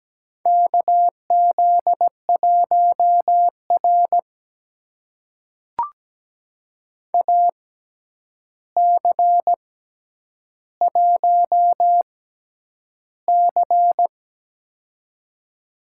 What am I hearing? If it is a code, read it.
Morse code, heard as a keyed tone, about 17 wpm, spaced as for 9 wpm, 700 Hz. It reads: KZ1R AC1C
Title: USA Call Signs - 4 Characters - 6x Character Spacing - 17wpm